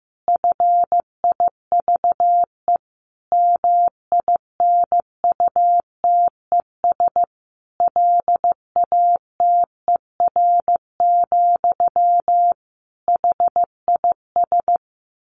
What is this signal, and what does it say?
Morse code, heard as a keyed tone, about 15 wpm, 700 Hz. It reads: FIVE MINUTES LATER, HIS
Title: A Sound Of Thunder 15wpm